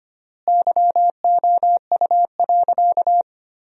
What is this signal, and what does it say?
Morse code, heard as a keyed tone, about 25 wpm, 700 Hz. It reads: YOU.
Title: 3x Word-Spacing - Sentences from Top 200 Words - 25wpm